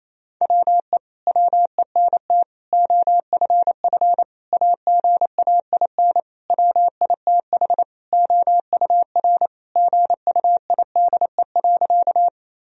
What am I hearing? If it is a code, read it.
Morse code, heard as a keyed tone, about 28 wpm, 700 Hz. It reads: WE WENT OFF AGAIN WITH OUR GUIDE.
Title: Sentences from Top 1000 Words - Review Part 3 - 28wpm